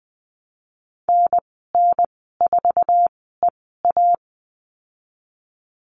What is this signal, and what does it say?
Morse code, heard as a keyed tone, about 20 wpm, spaced as for 15 wpm, 700 Hz. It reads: NN4EA
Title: 082 Lesson 14g - Number 2 - Call Signs 20wpm